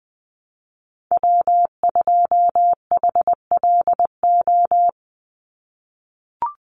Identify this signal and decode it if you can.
Morse code, heard as a keyed tone, about 20 wpm, 700 Hz. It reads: W2HLO
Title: USA Call Signs 20wpm